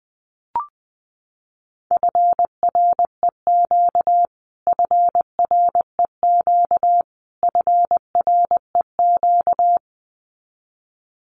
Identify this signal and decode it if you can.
Morse code, heard as a keyed tone, about 20 wpm, 700 Hz. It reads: FREQ FREQ FREQ
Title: Abbreviations Repeated 3x - 20wpm